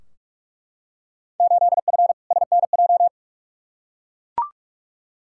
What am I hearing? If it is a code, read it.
Morse code, heard as a keyed tone, about 47 wpm, 700 Hz. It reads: 8F SNJ
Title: Field Day Exchanges Encore - 45wpm